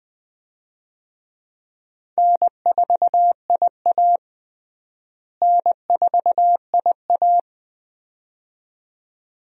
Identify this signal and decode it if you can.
Morse code, heard as a keyed tone, about 20 wpm, 700 Hz. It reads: N4IA N4IA
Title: USA Call Signs - 4 Characters - Repeated 2x - 20wpm